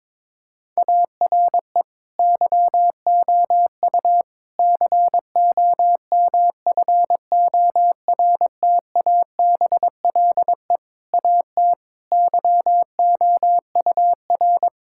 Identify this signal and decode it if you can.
Morse code, heard as a keyed tone, about 22 wpm, 700 Hz. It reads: ARE YOU COMFORTABLE AT YOUR
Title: Sentences from Top 1600 Words 22wpm